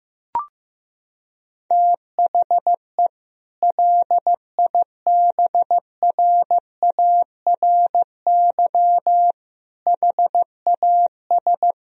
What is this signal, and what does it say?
Morse code, heard as a keyed tone, about 15 wpm, 700 Hz. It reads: THE LIBRARY HAS
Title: Sentences from Top 1000 Words - Review Part 4 - 15wpm